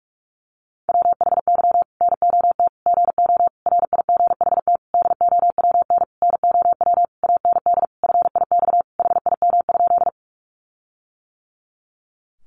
Morse code, heard as a keyed tone, about 44 wpm, 700 Hz. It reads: WHY NOT GO RIGHT DOWN NOW AND FIX HIM?
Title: Sentences from Top 1400 Words 45wpm